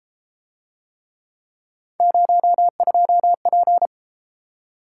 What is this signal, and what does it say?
Morse code, heard as a keyed tone, about 33 wpm, 700 Hz. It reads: 02P